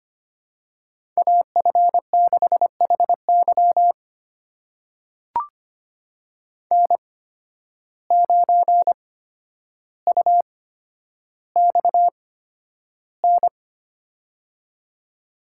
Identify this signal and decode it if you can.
Morse code, heard as a keyed tone, about 25 wpm, spaced as for 11 wpm, 700 Hz. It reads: AF6HY N9UXN
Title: USA Call Signs - 5 Characters - 8x Character Spacing - 25wpm